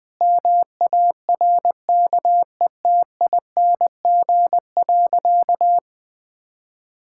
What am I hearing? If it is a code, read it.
Morse code, heard as a keyed tone, about 21 wpm, 700 Hz. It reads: MARKETING.
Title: Sentences from Top 1000 Words - Review Part 6 - 20wpm